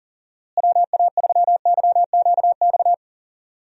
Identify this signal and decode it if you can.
Morse code, heard as a keyed tone, about 40 wpm, 700 Hz. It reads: WA3YQX
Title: USA Call Signs - 6 Characters - Repeated 2x - 40wpm